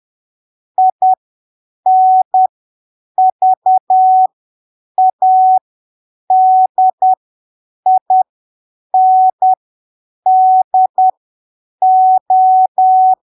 Morse code, heard as a keyed tone, about 10 wpm, spaced as for 7 wpm, 750 Hz. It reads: INVADINDO
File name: arquivos / texto / texto 25 10ppm